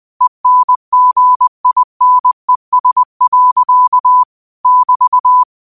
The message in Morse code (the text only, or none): ENGINES. <BT>